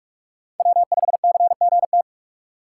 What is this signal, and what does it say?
Morse code, heard as a keyed tone, about 46 wpm, 700 Hz. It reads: W5CGT
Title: USA Call Signs - 5 Characters - 4x Character Spacing - 45wpm